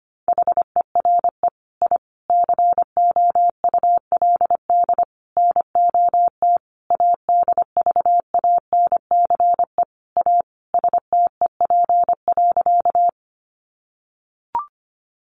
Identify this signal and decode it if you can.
Morse code, heard as a keyed tone, about 25 wpm, 700 Hz. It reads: HERE I COULD NOT ADVANCE A STEP.
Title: Sentences from Top 1400 Words 25wpm